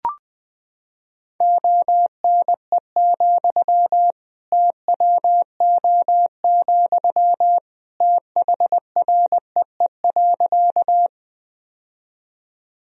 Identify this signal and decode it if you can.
Morse code, heard as a keyed tone, about 19 wpm, 700 Hz. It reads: ONE, TWO, THREE.
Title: Sentences from Top 1000 Words - Review Part 1 - 20wpm